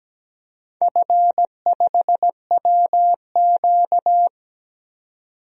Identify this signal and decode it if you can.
Morse code, heard as a keyed tone, about 17 wpm, 700 Hz. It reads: F5WQ
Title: DXCC-100 Call Signs - 4 Characters - 17wpm